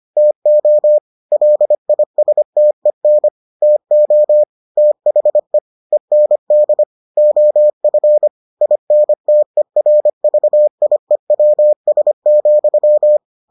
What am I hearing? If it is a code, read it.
Morse code, heard as a keyed tone, about 25 wpm, 600 Hz. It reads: TO LISTEN TO THE END OF INTERVIEWS,